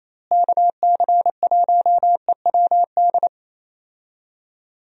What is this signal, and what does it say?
Morse code, heard as a keyed tone, about 28 wpm, 700 Hz. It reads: KC1EWD
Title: USA Call Signs - 6 Characters - Repeated 2x - 28wpm